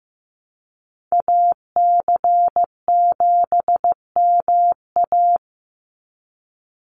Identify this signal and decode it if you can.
Morse code, heard as a keyed tone, about 15 wpm, 700 Hz. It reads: AC7MA
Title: NA SOTA Activator Callsigns 15wpm